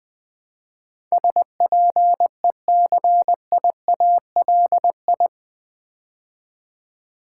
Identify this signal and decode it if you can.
Morse code, heard as a keyed tone, about 20 wpm, 700 Hz. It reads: SPECIALI
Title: Build A Word - 4 Letters At A Time - 10-Letter Words - 20wpm